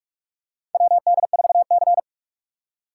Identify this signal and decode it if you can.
Morse code, heard as a keyed tone, about 44 wpm, 690 Hz. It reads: WD4C
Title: USA Call Signs - 4 Characters - 4x Character Spacing - 45wpm